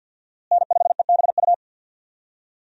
Morse code, heard as a keyed tone, about 49 wpm, 700 Hz. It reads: N5EBU
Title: USA Call Signs - 5 Characters - Repeated 3x - 50wpm